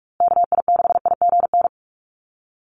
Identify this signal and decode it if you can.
Morse code, heard as a keyed tone, about 44 wpm, 700 Hz. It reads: KI6IGN
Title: USA Call Signs Repeated 2x 45wpm